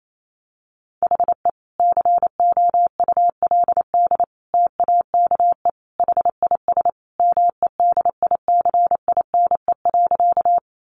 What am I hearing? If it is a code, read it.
Morse code, heard as a keyed tone, about 28 wpm, 700 Hz. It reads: HE COULD TAKE HIS MEDICINE.
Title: Sentences from Top 900 Words 28wpm